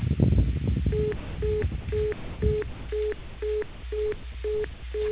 Telephone call progress tone (425 Hz), heard as a telephone network's tone, about 420 Hz. Congestion tone.